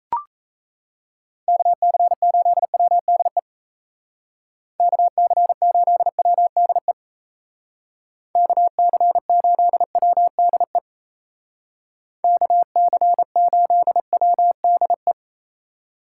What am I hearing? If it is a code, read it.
Morse code, heard as a keyed tone, about 37 wpm, 700 Hz. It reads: KC8WDE KC8WDE KC8WDE KC8WDE